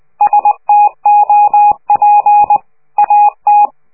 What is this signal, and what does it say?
Morse code, heard as a keyed tone, about 20 wpm, 830 Hz. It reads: STOP AT
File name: worldcoastal / scheveningenradio / morsepch